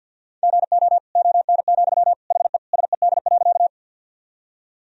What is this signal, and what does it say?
Morse code, heard as a keyed tone, about 47 wpm, 700 Hz. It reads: GO ON, HE SED.
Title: Sentences from Top 100 Words and Abbreviations 50wpm